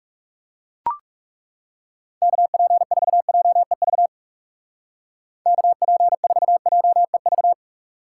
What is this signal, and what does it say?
Morse code, heard as a keyed tone, about 37 wpm, 700 Hz. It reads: KP4JEV KP4JEV
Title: Speed Racing - USA 2x3 Call Signs 30wpm